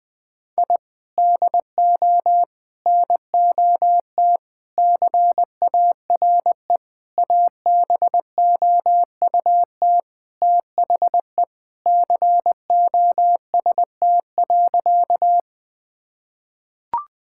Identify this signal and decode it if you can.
Morse code, heard as a keyed tone, about 20 wpm, 700 Hz. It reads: I DO NOT CARE ABOUT THE COST.
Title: Sentences from Top 1000 Words - Review Part 5 - 20wpm